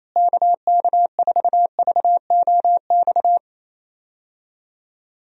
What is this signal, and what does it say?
Morse code, heard as a keyed tone, about 28 wpm, 700 Hz. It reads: KK4VOX
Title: USA Call Signs - 6 Characters - Repeated 2x - 28wpm